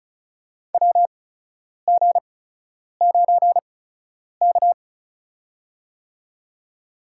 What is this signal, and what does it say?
Morse code, heard as a keyed tone, about 35 wpm, spaced as for 10 wpm, 700 Hz. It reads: WG9K
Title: USA Call Signs - 8x Character Spacing - 35wpm